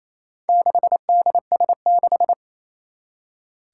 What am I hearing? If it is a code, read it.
Morse code, heard as a keyed tone, about 27 wpm, 700 Hz. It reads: 6DS6
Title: Four Challenging Letter-Number Combinations 28wpm